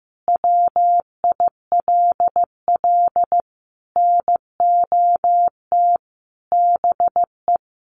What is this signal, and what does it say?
Morse code, heard as a keyed tone, about 15 wpm, 700 Hz. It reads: WILL NOT BE
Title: Sentences from Top 1200 Words 15wpm